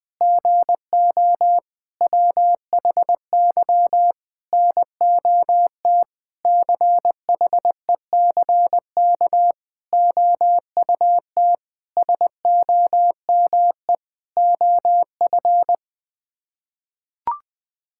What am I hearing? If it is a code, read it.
Morse code, heard as a keyed tone, about 20 wpm, 700 Hz. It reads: GO WHY NOT CHECK OUT SOME OF
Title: Sets of 9 Words - 20wpm